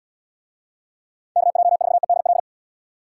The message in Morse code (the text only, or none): KJ4EDL